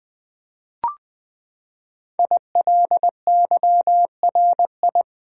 Morse code, heard as a keyed tone, about 20 wpm, 700 Hz. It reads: ILYRI
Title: Code Groups - 5 Letters - 20wpm